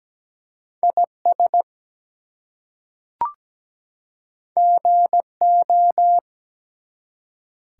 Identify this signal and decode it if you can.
Morse code, heard as a keyed tone, about 17 wpm, 700 Hz. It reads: IS GO
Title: Two Letter Words - 17wpm